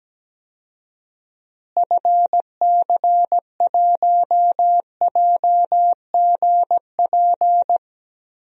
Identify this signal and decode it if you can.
Morse code, heard as a keyed tone, about 17 wpm, 700 Hz. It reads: FC1JGP
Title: DXCC-100 Call Signs - 6 Characters - Repeated 3x - 17wpm